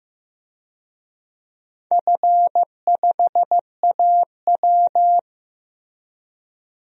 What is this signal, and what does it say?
Morse code, heard as a keyed tone, about 15 wpm, 700 Hz. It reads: F5AW
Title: DXCC-100 Call Signs - 4 Characters - Repeated 2x - 15wpm